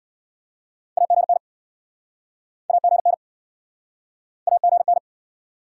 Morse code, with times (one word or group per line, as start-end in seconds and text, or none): 1.0-1.4 AGN
2.7-3.1 AGN
4.5-5.0 AGN